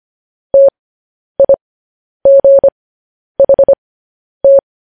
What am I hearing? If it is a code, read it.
Morse code, heard as a keyed tone, about 25 wpm, spaced as for 10 wpm, 550 Hz. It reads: TIGHT
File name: podcast / News headlines-1x-25x10